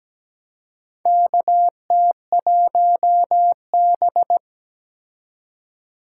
Morse code, heard as a keyed tone, about 17 wpm, 700 Hz. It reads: KT1B
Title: K1USN SST QSO Practice - 17wpm